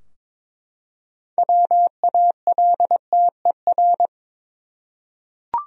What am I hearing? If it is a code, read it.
Morse code, heard as a keyed tone, about 22 wpm, 700 Hz. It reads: WALTER
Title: USA Common Names - 22wpm